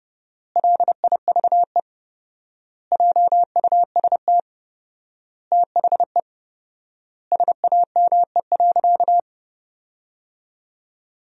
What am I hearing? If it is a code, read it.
Morse code, heard as a keyed tone, about 30 wpm, 700 Hz. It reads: LIVE JUST THE SAME.
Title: 4x Word-Spacing - Sentences from Top 300 Words - 30wpm